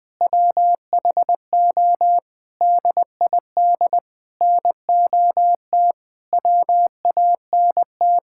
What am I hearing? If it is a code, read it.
Morse code, heard as a keyed tone, about 21 wpm, 700 Hz. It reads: WHO DID NOT WANT
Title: Sentences with 6-Word Phrases - Spoken Beginning - 20wpm